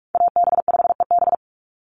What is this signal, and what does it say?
Morse code, heard as a keyed tone, about 44 wpm, 700 Hz. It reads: AB5EB
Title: Call Signs - 45wpm